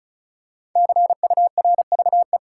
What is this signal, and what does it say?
Morse code, heard as a keyed tone, about 36 wpm, 700 Hz. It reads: CURVE
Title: Five Letter Words - 35wpm